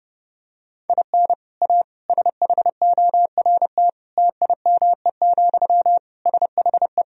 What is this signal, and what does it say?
Morse code, heard as a keyed tone, about 28 wpm, 700 Hz. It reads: IN A SHORT TIME, SHE
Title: Sentences from Top 1000 Words - Review Part 2 - 30wpm